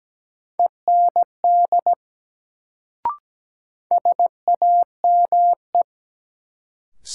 Morse code, heard as a keyed tone, about 17 wpm, 700 Hz. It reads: END SAME
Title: Sentences from Top 200 Words - New Word Drill - 17wpm